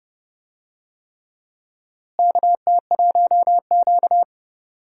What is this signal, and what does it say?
Morse code, heard as a keyed tone, about 30 wpm, 700 Hz. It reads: KT1Q
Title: USA Call Signs - 4 Characters - Repeated 3x - 30wpm